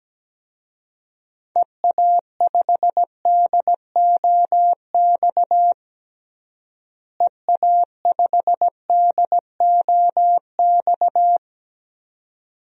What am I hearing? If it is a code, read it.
Morse code, heard as a keyed tone, about 17 wpm, 700 Hz. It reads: EA5DOX EA5DOX
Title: DXCC-100 Call Signs - 6 Characters - Repeated 2x - 17wpm